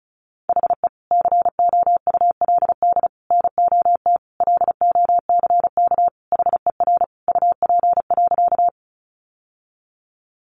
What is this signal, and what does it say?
Morse code, heard as a keyed tone, about 34 wpm, 700 Hz. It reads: HE COULD NOT LOCK HER UP.
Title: Sentences from Top 1300 Words 35wpm